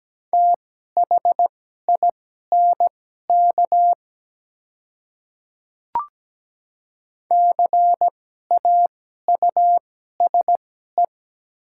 Morse code, heard as a keyed tone, about 17 wpm, spaced as for 12 wpm, 700 Hz. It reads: THINK CAUSE
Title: 2x Character Spacing - Top 200 Words - 17wpm